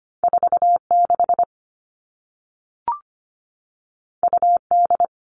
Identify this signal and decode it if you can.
Morse code, heard as a keyed tone, about 25 wpm, 700 Hz. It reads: 46 UD
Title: All Cut Numbers - 2-Digit 25wpm